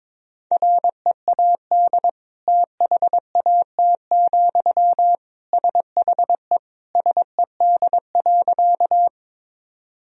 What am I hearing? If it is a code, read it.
Morse code, heard as a keyed tone, about 22 wpm, 700 Hz. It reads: READ THAT, SHE SED.
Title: Sentences from Top 200 Words and Abbreviations 22wpm